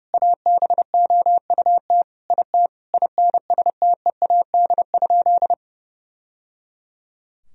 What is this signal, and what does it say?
Morse code, heard as a keyed tone, about 32 wpm, 700 Hz. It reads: ABOUT IT INSTEAD?
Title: Sentences from Top 1000 Words - Review Part 4 - 30wpm